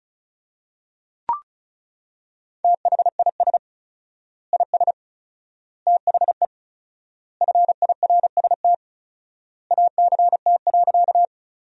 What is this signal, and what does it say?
Morse code, heard as a keyed tone, about 36 wpm, 700 Hz. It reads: THIS IS THE FIRST ACT.
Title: 4x Word-Spacing - Sentences from Top 200 Words - 35wpm